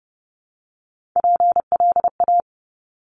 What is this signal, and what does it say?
Morse code, heard as a keyed tone, about 30 wpm, 690 Hz. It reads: PLA
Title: Top Three Letter Combinations in Words 30wpm